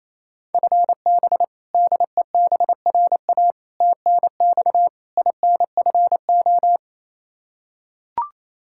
Morse code, heard as a keyed tone, about 28 wpm, 700 Hz. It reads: FB DEBRA TNX INFO
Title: Practice QSO - Exchange 3 - More Info - 28wpm